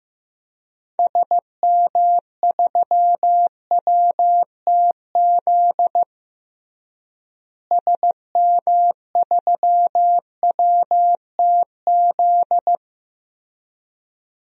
Morse code, heard as a keyed tone, about 15 wpm, 700 Hz. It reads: SM3WTZ SM3WTZ
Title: DXCC-100 Call Signs - 6 Characters - Repeated 2x - 15wpm